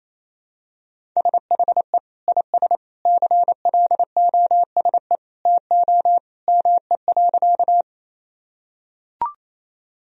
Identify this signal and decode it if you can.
Morse code, heard as a keyed tone, about 28 wpm, 700 Hz. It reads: SHE IS CLOSE TO ME.
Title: Sentences from Top 1000 Words - Review Part 1 - 28wpm